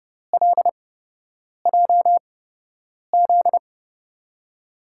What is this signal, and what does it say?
Morse code, heard as a keyed tone, about 30 wpm, spaced as for 8 wpm, 700 Hz. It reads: LJZ
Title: USA Call Signs - 5 Characters - 8x Character Spacing - 30wpm